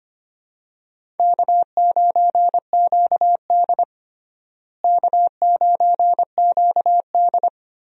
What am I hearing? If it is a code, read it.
Morse code, heard as a keyed tone, about 25 wpm, 700 Hz. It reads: K9QD K9QD
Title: USA Call Signs - 4 Characters - Repeated 2x - 25wpm